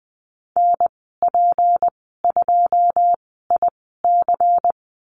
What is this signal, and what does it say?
Morse code, heard as a keyed tone, about 20 wpm, spaced as for 15 wpm, 700 Hz. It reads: NP2IC
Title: 179 Lesson 29g - Slash - Call Signs 20wpm